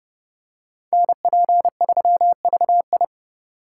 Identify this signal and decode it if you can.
Morse code, heard as a keyed tone, about 28 wpm, 700 Hz. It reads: NP3VI